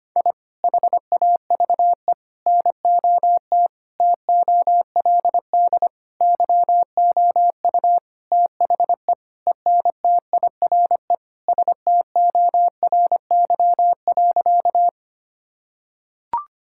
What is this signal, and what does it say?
Morse code, heard as a keyed tone, about 25 wpm, 700 Hz. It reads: I HAVE NOT TOLD YOU THE ENTIRE STORY.
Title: Sentences from Top 1000 Words - Review Part 5 - 25wpm